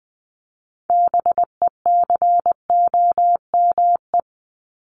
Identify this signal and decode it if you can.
Morse code, heard as a keyed tone, about 20 wpm, 700 Hz. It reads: BECOME
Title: Top 501-750 Words 20wpm